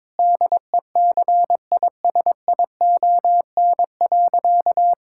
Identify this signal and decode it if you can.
Morse code, heard as a keyed tone, about 22 wpm, 700 Hz. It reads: DECISION.
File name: Difficult Sentences with Hints - 22wpm